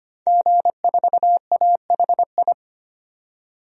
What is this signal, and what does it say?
Morse code, heard as a keyed tone, about 25 wpm, 700 Hz. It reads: G4AHI
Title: DXCC-100 Call Signs - 5 Characters - 25wpm